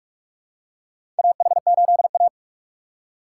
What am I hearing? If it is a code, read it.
Morse code, heard as a keyed tone, about 46 wpm, 700 Hz. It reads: AH8A
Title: USA Call Signs - 4 Characters - 6x Character Spacing - 45wpm